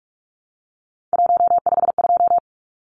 Morse code, heard as a keyed tone, about 44 wpm, 700 Hz. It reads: 152
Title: Numbers - 3-Digit 45wpm